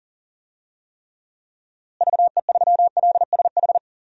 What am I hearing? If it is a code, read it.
Morse code, heard as a keyed tone, about 37 wpm, 700 Hz. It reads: VE3LSH
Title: DXCC-100 Call Signs - 6 Characters - Repeated 3x - 40wpm